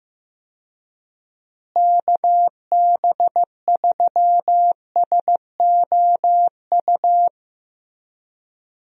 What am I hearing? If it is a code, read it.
Morse code, heard as a keyed tone, about 14 wpm, 700 Hz. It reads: KB3SOU
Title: USA Call Signs - 6 Characters - Repeated 3x - 15wpm